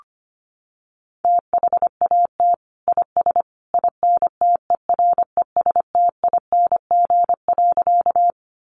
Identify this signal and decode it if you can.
Morse code, heard as a keyed tone, about 25 wpm, 700 Hz. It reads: THAT IS INTERESTING.
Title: Sentences from Top 1000 Words 25wpm